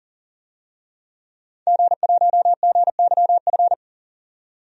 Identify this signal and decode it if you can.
Morse code, heard as a keyed tone, about 40 wpm, 700 Hz. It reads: G1GYF
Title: DXCC-100 Call Signs - 5 Characters - 40wpm